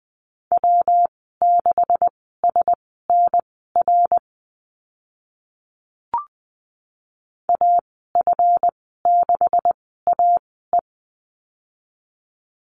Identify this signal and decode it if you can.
Morse code, heard as a keyed tone, about 20 wpm, spaced as for 15 wpm, 700 Hz. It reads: W6SNR AF6AE
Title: 140 Lesson 23g - Letter f - Call Signs 20wpm